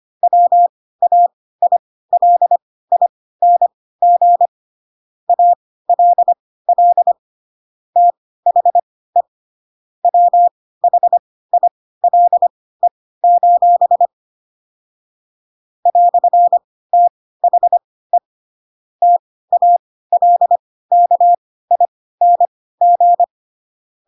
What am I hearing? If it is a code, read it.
Morse code, heard as a keyed tone, about 25 wpm, spaced as for 16 wpm, 700 Hz. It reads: WAILING ALL THE WHILE: "THE TALKING